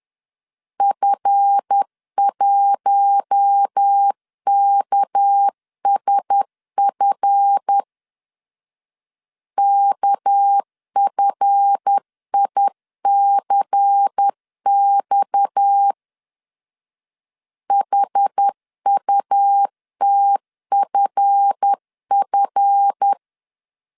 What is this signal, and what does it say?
Morse code, heard as a keyed tone, about 10 wpm, 800 Hz. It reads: F1KSF KFICX HUTFF
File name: LEZIONI / 7 GF9Z / 7 GF 10wpm 2514